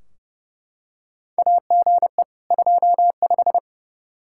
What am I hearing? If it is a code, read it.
Morse code, heard as a keyed tone, about 31 wpm, 700 Hz. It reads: AGE 25
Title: Practice QSO - Exchange 3 - More Info - 30wpm